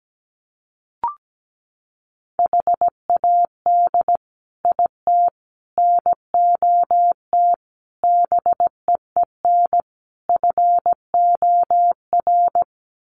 Morse code, heard as a keyed tone, about 17 wpm, 700 Hz. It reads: HAD IT NOT BEEN FOR
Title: Sets of 5 Words Encore2 17wpm